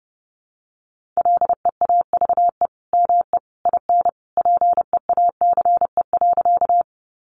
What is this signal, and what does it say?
Morse code, heard as a keyed tone, about 30 wpm, 700 Hz. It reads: LEAVE ME IN PEACE.
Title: Sentences from Top 1500 Words 30wpm